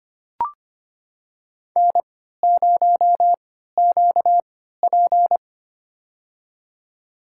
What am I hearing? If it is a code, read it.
Morse code, heard as a keyed tone, about 25 wpm, spaced as for 14 wpm, 700 Hz. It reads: N0QP
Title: USA Call Signs - 3x Character Spacing - 25wpm